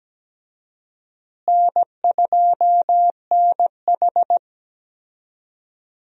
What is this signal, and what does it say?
Morse code, heard as a keyed tone, about 17 wpm, 700 Hz. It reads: N2NH
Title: USA Call Signs - 4 Characters - Repeated 2x - 17wpm